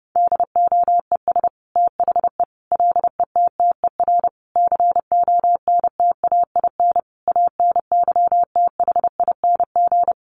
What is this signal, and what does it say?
Morse code, heard as a keyed tone, about 29 wpm, 700 Hz. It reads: DOES THE LETTER CONTAIN ANYTHING
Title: Sentences from Top 1100 Words 30wpm